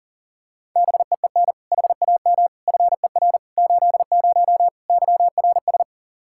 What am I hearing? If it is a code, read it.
Morse code, heard as a keyed tone, about 40 wpm, 700 Hz. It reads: BEEN HAM FER 80 YRS
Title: K1USN SST QSO Practice 40wpm